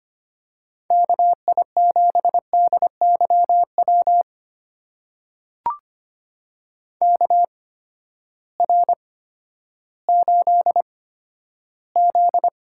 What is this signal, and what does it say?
Morse code, heard as a keyed tone, about 25 wpm, spaced as for 12 wpm, 700 Hz. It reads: KI7DYW KR8Z
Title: USA Call Signs - 8x Character Spacing - 25wpm